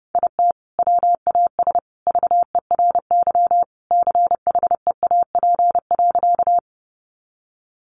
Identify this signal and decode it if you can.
Morse code, heard as a keyed tone, about 30 wpm, 700 Hz. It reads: IT WAS VERY CHEAP.
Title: Sentences from Top 1200 Words 30wpm